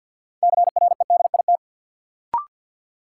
Morse code, heard as a keyed tone, about 50 wpm, 700 Hz. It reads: CREDIT
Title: Sentences from Top 1000 Words - New Word Drill - 50wpm